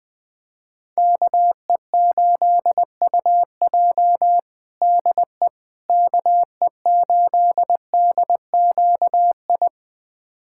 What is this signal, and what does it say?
Morse code, heard as a keyed tone, about 20 wpm, 700 Hz. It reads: KE8UJ DE KE8DQI
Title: USA Callsign DE Callsign 20wpm